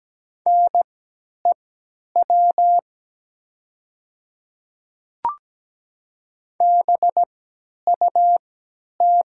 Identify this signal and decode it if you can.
Morse code, heard as a keyed tone, about 17 wpm, spaced as for 10 wpm, 700 Hz. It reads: NEW BUT
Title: 3x Character Spacing - Top 300 Words - 17wpm